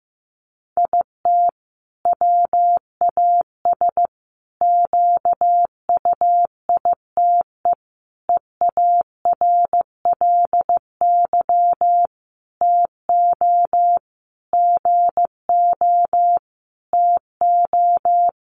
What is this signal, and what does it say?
Morse code, heard as a keyed tone, about 15 wpm, 700 Hz. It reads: IT WAS QUITE EARLY TO GO TO
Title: Sentences from Top 700 Words 15wpm